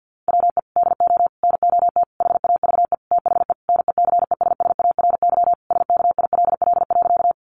Morse code, heard as a keyed tone, about 50 wpm, 700 Hz. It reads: WE DO NOT HAVE THE NECESSARY SKILL.
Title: Sentences from Top 1200 Words 50wpm